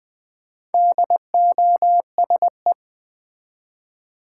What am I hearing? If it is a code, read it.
Morse code, heard as a keyed tone, about 20 wpm, 690 Hz. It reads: DOSE
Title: Build A Word - 2 Letters At A Time - 4-Letter Words - 20wpm